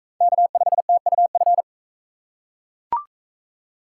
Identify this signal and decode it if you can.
Morse code, heard as a keyed tone, about 42 wpm, 700 Hz. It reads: K5TUF